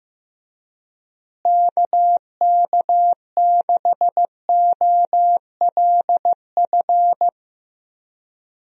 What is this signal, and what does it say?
Morse code, heard as a keyed tone, about 15 wpm, 700 Hz. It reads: KK6OLF
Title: USA Call Signs - 6 Characters - Repeated 3x - 15wpm